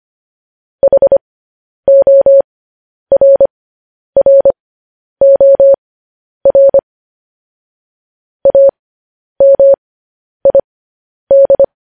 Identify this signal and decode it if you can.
Morse code, heard as a keyed tone, about 25 wpm, spaced as for 10 wpm, 550 Hz. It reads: HORROR AMID
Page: news-headlines-in-morse-code-at-10-wpm